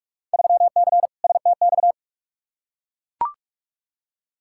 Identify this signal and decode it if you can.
Morse code, heard as a keyed tone, about 42 wpm, 700 Hz. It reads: 3C STX
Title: Field Day Exchanges Encore - 45wpm